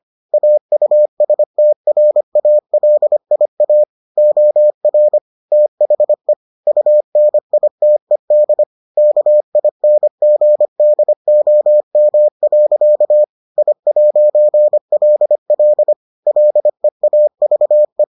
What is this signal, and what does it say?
Morse code, heard as a keyed tone, about 25 wpm, 600 Hz. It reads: AUSTRALIA OR THE UNITED KINGDOM. I'LL LEAVE